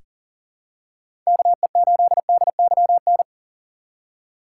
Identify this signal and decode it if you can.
Morse code, heard as a keyed tone, about 40 wpm, 700 Hz. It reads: KE8DYN
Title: USA Call Signs - 4x Character Spacing - 40wpm